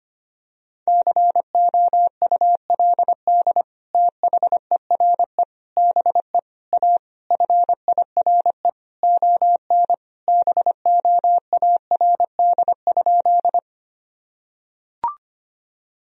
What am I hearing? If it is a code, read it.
Morse code, heard as a keyed tone, about 25 wpm, 700 Hz. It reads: COULD THERE BE A FIRE ON BOARD?
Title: Sentences from Top 1000 Words - Review Part 4 - 25wpm